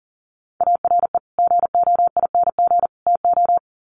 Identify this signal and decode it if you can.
Morse code, heard as a keyed tone, about 40 wpm, 700 Hz. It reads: ARE GOING TO
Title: Sets of 3 Words 40wpm